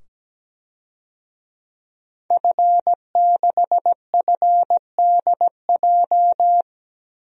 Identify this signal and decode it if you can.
Morse code, heard as a keyed tone, about 17 wpm, 700 Hz. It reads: F6FDJ
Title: DXCC-100 Call Signs - 5 Characters - 17wpm